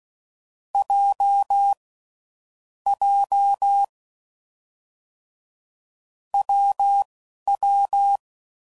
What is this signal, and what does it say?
Morse code, heard as a keyed tone, about 16 wpm, spaced as for 11 wpm, 800 Hz. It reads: J J WW
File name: Combinations / Difficult Combos / WPJ WPJ1